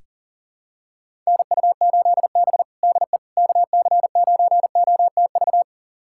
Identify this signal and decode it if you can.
Morse code, heard as a keyed tone, about 37 wpm, 700 Hz. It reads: NU8B DE KC9OTV
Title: USA Callsign DE Callsign 40wpm